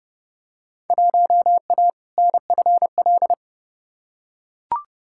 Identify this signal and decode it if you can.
Morse code, heard as a keyed tone, about 31 wpm, 700 Hz. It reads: 1A NFL
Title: Field Day Exchanges - 30wpm